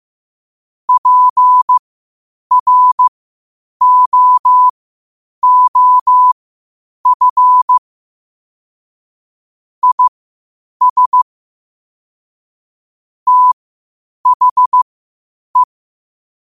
Morse code, heard as a keyed tone, about 15 wpm, spaced as for 8 wpm, 1000 Hz. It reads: PROOF IS THE